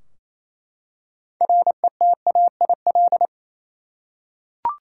Morse code, heard as a keyed tone, about 28 wpm, 700 Hz. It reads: RETAIL
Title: Practice QSO - Exchange 3 - More Info - 28wpm